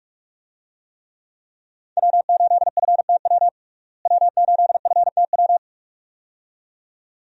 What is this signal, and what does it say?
Morse code, heard as a keyed tone, about 42 wpm, 700 Hz. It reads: W8FTW W8FTW
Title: USA Call Signs - 5 Characters - Repeated 2x - 45wpm